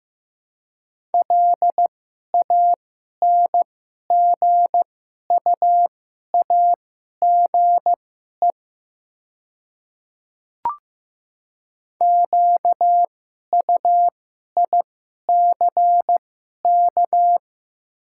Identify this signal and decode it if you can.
Morse code, heard as a keyed tone, about 15 wpm, spaced as for 11 wpm, 700 Hz. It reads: LANGUAGE QUICK
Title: 2x Character Spacing - Top 500 Words - 15wpm